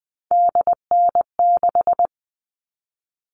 Morse code, heard as a keyed tone, about 20 wpm, 700 Hz. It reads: DN6
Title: All Cut Numbers - 3-Digit 20wpm